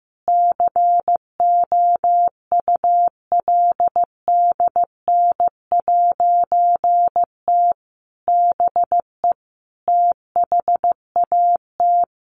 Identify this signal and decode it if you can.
Morse code, heard as a keyed tone, about 15 wpm, 700 Hz. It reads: COULDN'T BE THAT
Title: A Sound Of Thunder 15wpm